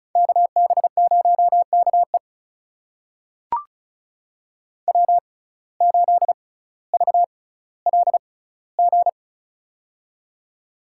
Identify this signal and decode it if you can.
Morse code, heard as a keyed tone, about 36 wpm, spaced as for 18 wpm, 700 Hz. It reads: KB0KE W8VLG
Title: USA Call Signs - 5 Characters - 6x Character Spacing - 35wpm